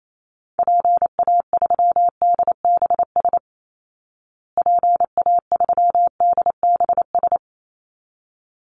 Morse code, heard as a keyed tone, about 28 wpm, 700 Hz. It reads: PA3DBS PA3DBS
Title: Call Signs Repeated 3x - 28wpm